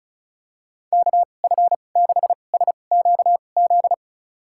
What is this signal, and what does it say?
Morse code, heard as a keyed tone, about 36 wpm, spaced as for 25 wpm, 700 Hz. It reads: KF6SQZ
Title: USA Call Signs - 6 Characters - 2x Character Spacing - 35wpm